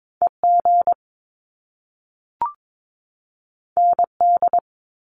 Morse code, heard as a keyed tone, about 22 wpm, 700 Hz. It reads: EG ND
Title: Top Two Letter Combinations in Words 22wpm